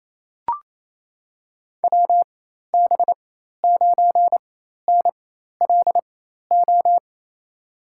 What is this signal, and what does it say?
Morse code, heard as a keyed tone, about 28 wpm, spaced as for 13 wpm, 700 Hz. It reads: WB9NLO
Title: USA Call Signs - 6 Characters - 4x Character Spacing - 28wpm